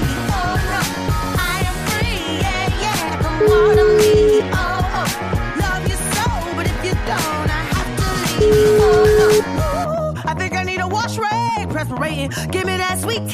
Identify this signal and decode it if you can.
Ringback tone (Europe, 425 Hz), with noise.